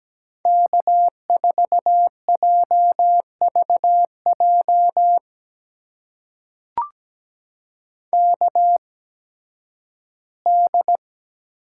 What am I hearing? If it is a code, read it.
Morse code, heard as a keyed tone, about 17 wpm, spaced as for 11 wpm, 700 Hz. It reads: K4JVJ KD
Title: USA Call Signs - 5 Characters - 8x Character Spacing - 17wpm